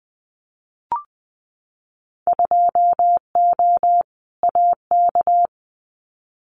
Morse code, heard as a keyed tone, about 20 wpm, 700 Hz. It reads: 2O AK
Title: Winter Field Day Exchanges Encore 20wpm